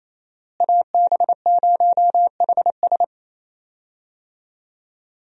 Morse code, heard as a keyed tone, about 28 wpm, 700 Hz. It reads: AB0HS
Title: USA Call Signs - 5 Characters - Repeated 2x - 28wpm